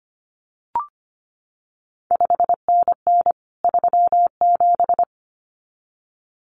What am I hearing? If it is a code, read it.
Morse code, heard as a keyed tone, about 25 wpm, 700 Hz. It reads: 5NN 37
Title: CQ WW DX Contest Exchange Encore - 25wpm